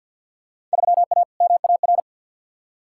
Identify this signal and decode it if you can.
Morse code, heard as a keyed tone, about 53 wpm, 700 Hz. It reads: 3A MAR
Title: Field Day Exchanges Encore - 50wpm